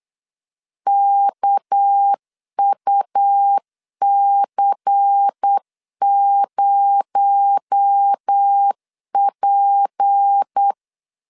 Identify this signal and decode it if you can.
Morse code, heard as a keyed tone, about 8 wpm, 800 Hz. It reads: KUC0P
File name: 8wpm